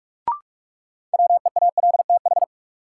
Morse code, heard as a keyed tone, about 46 wpm, 700 Hz. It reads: WEALTH
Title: Sentences from Top 1200 Words - New Word Drill Alt - 45wpm